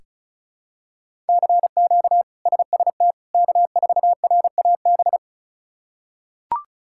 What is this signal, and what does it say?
Morse code, heard as a keyed tone, about 36 wpm, 700 Hz. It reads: CQ SST K4RAB
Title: K1USN SST QSO Practice - 35wpm